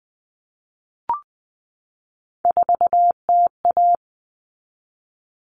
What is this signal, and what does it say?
Morse code, heard as a keyed tone, about 20 wpm, 700 Hz. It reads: 4TA
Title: All Cut Numbers - 3-Digit 20wpm